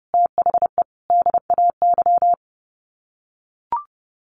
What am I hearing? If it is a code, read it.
Morse code, heard as a keyed tone, about 30 wpm, 700 Hz. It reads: THE DAY
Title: Sets of 3 Words 30wpm